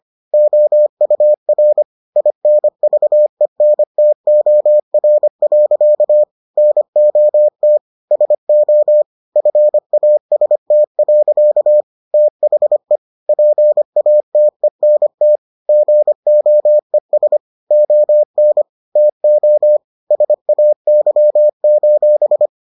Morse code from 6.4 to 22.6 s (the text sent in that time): NOT SO FAST. THE PATENT GOES ON TO SAY: